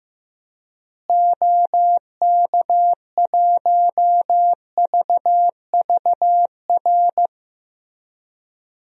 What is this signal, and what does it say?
Morse code, heard as a keyed tone, about 16 wpm, 700 Hz. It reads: OK1VVR
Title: DXCC-100 Call Signs - 6 Characters - Repeated 3x - 15wpm